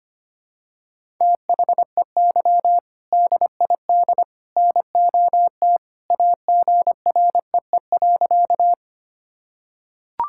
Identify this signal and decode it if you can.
Morse code, heard as a keyed tone, about 25 wpm, 700 Hz. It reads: THEY DID NOT AGREE.
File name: practice / 25wpm / Sentences with 3-Word Phrases - 25wpm